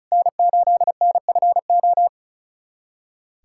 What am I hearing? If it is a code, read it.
Morse code, heard as a keyed tone, about 36 wpm, 690 Hz. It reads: N8NFO
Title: USA Callsign DE Callsign - 5 Characters - 35wpm